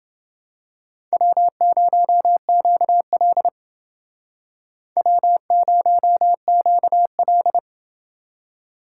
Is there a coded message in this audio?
Yes: Morse code from 1.1 to 7.6 s, heard as a keyed tone, about 28 wpm, 700 Hz. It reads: W0QL W0QL